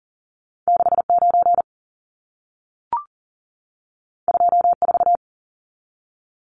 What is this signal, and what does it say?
Morse code, heard as a keyed tone, about 40 wpm, 700 Hz. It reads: 69 24
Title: Numbers - 2-Digit 40wpm